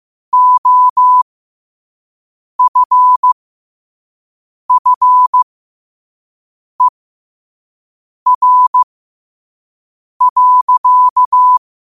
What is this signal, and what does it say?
Morse code, heard as a keyed tone, about 15 wpm, spaced as for 5 wpm, 1000 Hz. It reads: OFFER.
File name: qotdfiles / 5wpm